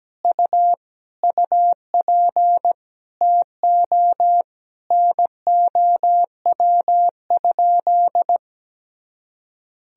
Morse code, heard as a keyed tone, about 17 wpm, 700 Hz. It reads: U UP TO NOW?
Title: Sentences from Top 100 Words and Abbreviations 17wpm